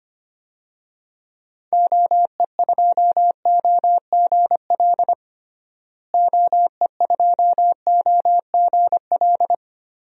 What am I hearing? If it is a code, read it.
Morse code, heard as a keyed tone, about 25 wpm, 700 Hz. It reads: OE2OGL OE2OGL
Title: DXCC-100 Call Signs - 6 Characters - Repeated 2x - 25wpm